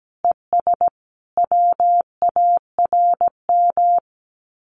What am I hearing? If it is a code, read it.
Morse code, heard as a keyed tone, about 17 wpm, 700 Hz. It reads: ES WARM
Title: Common QSO Phrases Encore 17wpm